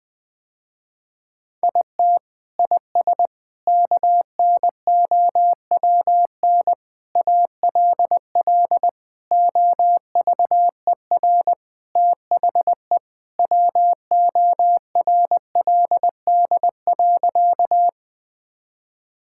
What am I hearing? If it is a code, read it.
Morse code, heard as a keyed tone, about 20 wpm, 700 Hz. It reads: IT IS KNOWN ALL OVER THE WORLD.
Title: Sentences with 4-Word Phrases - Spoken Beginning - 20wpm